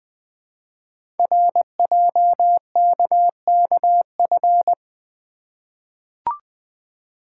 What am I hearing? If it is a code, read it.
Morse code, heard as a keyed tone, about 20 wpm, 700 Hz. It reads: RJKKF
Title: Code Groups - 5 Letters - 20wpm